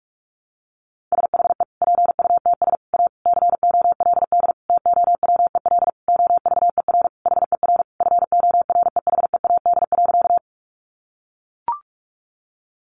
Morse code, heard as a keyed tone, about 44 wpm, 700 Hz. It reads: SHE PUTS A COLD TOWEL OVER HER FOREHEAD.
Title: Easy Phrases 45wpm